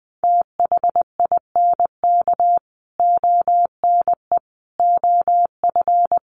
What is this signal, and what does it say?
Morse code, heard as a keyed tone, about 20 wpm, 700 Hz. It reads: THINK ONE OF
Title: Sets of 5 Words Encore2 20wpm